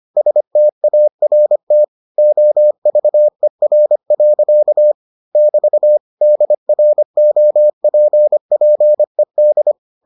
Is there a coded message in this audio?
Yes: Morse code from 0.2 to 9.7 s, heard as a keyed tone, about 25 wpm, 600 Hz. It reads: START OVER. <BT> DROPPED